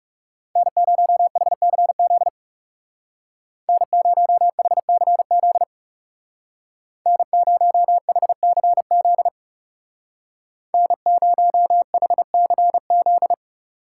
Morse code, heard as a keyed tone, about 36 wpm, 700 Hz. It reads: N0HCZ N0HCZ N0HCZ N0HCZ
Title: Speed Racing - USA 1x3 Call Signs 30wpm